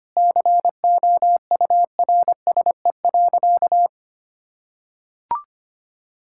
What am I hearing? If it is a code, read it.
Morse code, heard as a keyed tone, about 25 wpm, 700 Hz. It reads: COURSE.
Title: Sentences from Top 1000 Words - Review Part 5 - 25wpm